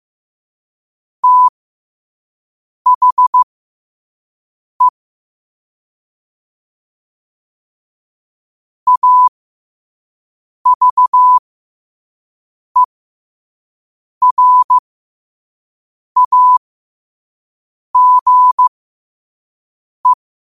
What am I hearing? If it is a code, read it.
Morse code, heard as a keyed tone, about 15 wpm, spaced as for 5 wpm, 1000 Hz. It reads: THE AVERAGE
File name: qotdfiles / 5wpm